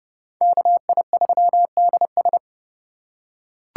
Morse code, heard as a keyed tone, about 28 wpm, 700 Hz. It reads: KI3DS